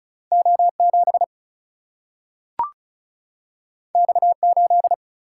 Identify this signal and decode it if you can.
Morse code, heard as a keyed tone, about 36 wpm, 700 Hz. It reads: O 7 X 8
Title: Single-Letter Single-Number 35wpm